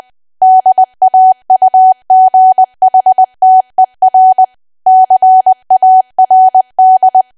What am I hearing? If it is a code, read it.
Morse code, heard as a keyed tone, about 20 wpm, 750 Hz. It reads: DAUGHTER CARD
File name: cwa / W1AW-Level3 / 20WPM